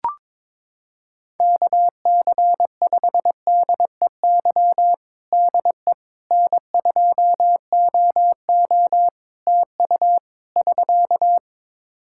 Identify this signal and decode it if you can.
Morse code, heard as a keyed tone, about 21 wpm, 700 Hz. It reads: KC5DEY DE N2OO TU <SK>
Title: K1USN SST QSO Practice 22wpm